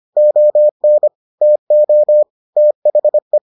Morse code, heard as a keyed tone, about 25 wpm, 600 Hz. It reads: ON TO THE